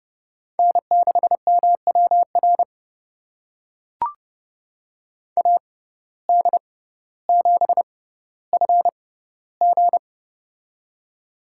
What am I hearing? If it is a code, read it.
Morse code, heard as a keyed tone, about 30 wpm, spaced as for 15 wpm, 700 Hz. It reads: N6MWR AD7FG
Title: USA Call Signs - 5 Characters - 6x Character Spacing - 30wpm